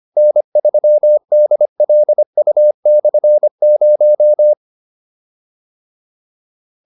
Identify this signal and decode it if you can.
Morse code, heard as a keyed tone, about 25 wpm, 600 Hz. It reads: N3DLU/0
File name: QSO305 25